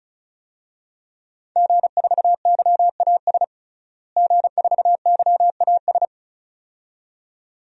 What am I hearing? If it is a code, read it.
Morse code, heard as a keyed tone, about 36 wpm, 700 Hz. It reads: G4YAS G4YAS
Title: DXCC-100 Call Signs - 5 Characters - Repeated 2x - 35wpm